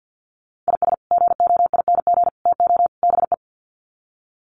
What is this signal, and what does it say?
Morse code, heard as a keyed tone, about 50 wpm, 700 Hz. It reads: IS GOING TO BE
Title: Sets of 4 Words 50wpm